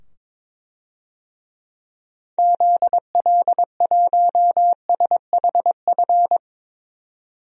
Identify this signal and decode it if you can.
Morse code, heard as a keyed tone, about 22 wpm, 700 Hz. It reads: ZL1SHF
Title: DXCC-100 Call Signs - 6 Characters - 22wpm